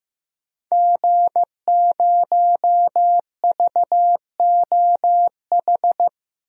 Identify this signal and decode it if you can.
Morse code, heard as a keyed tone, about 15 wpm, 700 Hz. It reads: G0VOH
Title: DXCC-100 Call Signs - 5 Characters - 15wpm